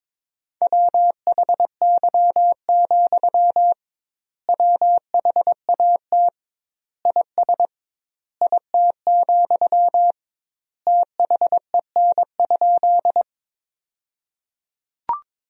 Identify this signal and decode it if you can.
Morse code, heard as a keyed tone, about 22 wpm, 700 Hz. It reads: WHY, WHAT IS IT, THEN?
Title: 2x Word-Spacing - Sentences from Top 200 Words - 22wpm